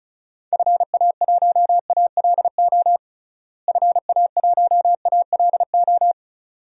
Morse code, heard as a keyed tone, about 36 wpm, 700 Hz. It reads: FA1ALO FA1ALO
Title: DXCC-100 Call Signs - 6 Characters - Repeated 2x - 35wpm